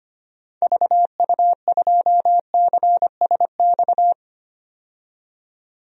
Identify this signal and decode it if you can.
Morse code, heard as a keyed tone, about 25 wpm, 700 Hz. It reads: VU2CSX
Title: DXCC-100 Call Signs - 6 Characters - 25wpm